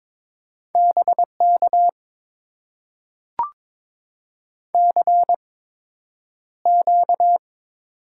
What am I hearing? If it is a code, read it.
Morse code, heard as a keyed tone, about 22 wpm, spaced as for 9 wpm, 690 Hz. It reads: BK CQ
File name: Top 90 QSO Elements - 8x Character Spacing - 22wpm